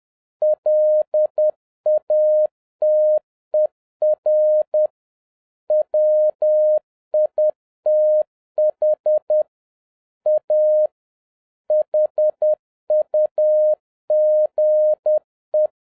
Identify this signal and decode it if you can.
Morse code, heard as a keyed tone, about 10 wpm, 610 Hz. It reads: LATER WITH A HUGE
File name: 2011-07-mp3 / n56 BACK FROM A BIG ACCIDENT 10wpm